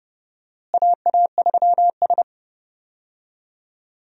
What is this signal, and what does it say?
Morse code, heard as a keyed tone, about 30 wpm, 700 Hz. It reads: AA3S
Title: USA Call Signs - 4 Characters - Repeated 2x - 30wpm